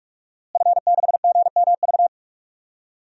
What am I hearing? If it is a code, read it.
Morse code, heard as a keyed tone, about 44 wpm, 700 Hz. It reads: F6GMV
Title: DXCC-100 Call Signs - 5 Characters - Repeated 3x - 45wpm